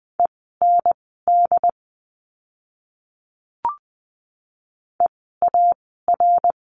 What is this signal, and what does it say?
Morse code, heard as a keyed tone, about 20 wpm, spaced as for 14 wpm, 700 Hz. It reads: END EAR
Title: 068 Lesson 12f - Letter d - Words 20wpm